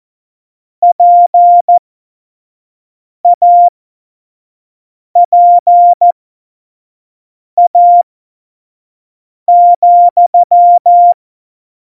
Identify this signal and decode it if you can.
Morse code, heard as a keyed tone, about 14 wpm, spaced as for 5 wpm, 700 Hz. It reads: PAPA,
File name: Morsekurs / Witze / Witz 08. August 17, Tempo 25 F70